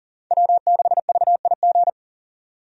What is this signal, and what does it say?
Morse code, heard as a keyed tone, about 38 wpm, 700 Hz. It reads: W6VIG
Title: USA Call Signs - 5 Characters - 40wpm